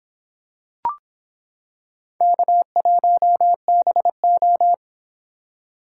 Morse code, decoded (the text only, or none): K1BO